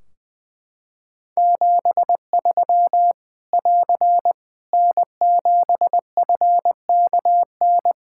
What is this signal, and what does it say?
Morse code, heard as a keyed tone, about 21 wpm, 700 Hz. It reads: 73 <AR> N7FKN
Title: Practice QSO - Exchange 4 - Ending - 20wpm